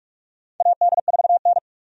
Morse code, heard as a keyed tone, about 46 wpm, 700 Hz. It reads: AD4N
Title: USA Call Signs - 4 Characters - 45wpm